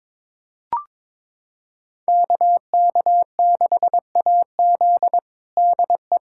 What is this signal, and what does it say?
Morse code, heard as a keyed tone, about 21 wpm, 700 Hz. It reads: KK6AZ DE
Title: USA Callsign DE Callsign 22wpm